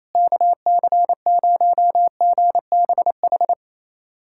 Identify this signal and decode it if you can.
Morse code, heard as a keyed tone, about 28 wpm, 700 Hz. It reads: KC0GBH
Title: USA Callsign DE Callsign 28wpm